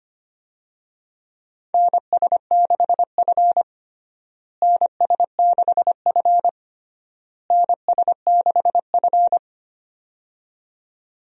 Morse code, heard as a keyed tone, about 25 wpm, 700 Hz. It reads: NS6F NS6F NS6F
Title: USA Call Signs - 4 Characters - Repeated 3x - 25wpm